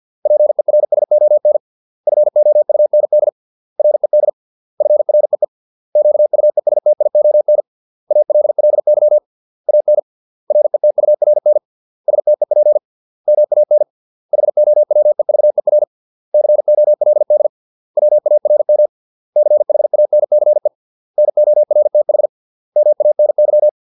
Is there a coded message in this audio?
Yes: Morse code from 0.3 to 23.7 s, heard as a keyed tone, about 50 wpm, 600 Hz. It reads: PERSON FOUND RED FREE QUESTION ALLY AN RETURN STEP MAN HOWEVER COLD WARM CHANCE NORTH MANY